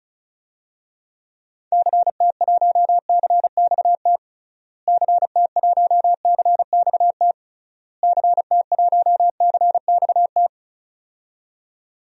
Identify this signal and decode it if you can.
Morse code, heard as a keyed tone, about 36 wpm, 700 Hz. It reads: CT1CXT CT1CXT CT1CXT
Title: DXCC-100 Call Signs - 6 Characters - Repeated 3x - 35wpm